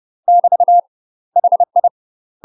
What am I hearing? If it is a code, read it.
Morse code, heard as a keyed tone, about 30 wpm, 700 Hz. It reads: <BT> HI
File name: UK MASTER QSOs TRAINING / 30WPM / UK MASTER QSOs 0022